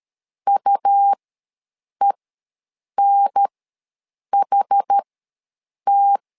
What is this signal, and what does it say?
Morse code, heard as a keyed tone, about 13 wpm, spaced as for 7 wpm, 800 Hz. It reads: UENHT
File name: LEZIONI / 2 UHDN / 2 UHDN ws 12wpm 1708